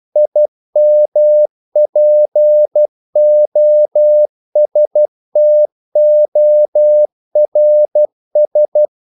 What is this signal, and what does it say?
Morse code, heard as a keyed tone, about 12 wpm, 600 Hz. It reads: IMPOSTORS
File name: CW / Practice / If12